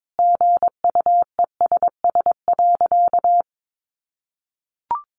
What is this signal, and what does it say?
Morse code, heard as a keyed tone, about 22 wpm, 700 Hz. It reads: GUESS.
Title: Sentences from Top 1500 Words 22wpm